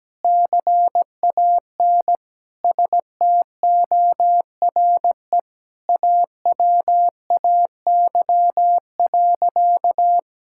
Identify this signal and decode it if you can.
Morse code, heard as a keyed tone, about 17 wpm, 700 Hz. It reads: CAN STORE AWAY.
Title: Sentences from Top 1000 Words - Review Part 5 - 17wpm